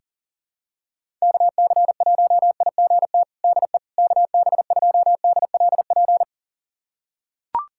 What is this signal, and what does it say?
Morse code, heard as a keyed tone, about 40 wpm, 700 Hz. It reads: KC1IGT DE KB2DLP
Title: USA Callsign DE Callsign - 6 Characters - 40wpm